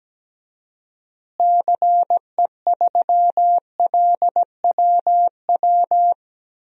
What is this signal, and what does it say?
Morse code, heard as a keyed tone, about 17 wpm, 700 Hz. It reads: CE3LWW
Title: DXCC-100 Call Signs - 6 Characters - Repeated 3x - 17wpm